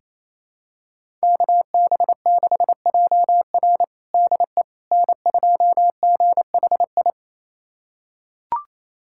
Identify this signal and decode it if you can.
Morse code, heard as a keyed tone, about 28 wpm, 700 Hz. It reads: KB6JR DE N2GHI
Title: USA Callsign DE Callsign 28wpm